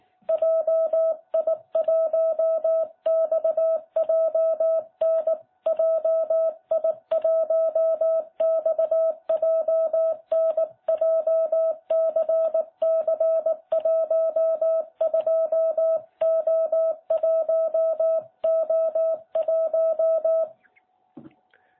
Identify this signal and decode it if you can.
Morse code, heard as a keyed tone, about 18 wpm, 660 Hz. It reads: JI1XJN JI1XJN JCC12O1O1